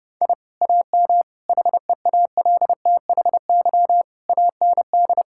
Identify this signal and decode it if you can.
Morse code, heard as a keyed tone, about 30 wpm, 700 Hz. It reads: I AM HEALTHY AND
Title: Sentences from Top 1000 Words - Review Part 5 - 30wpm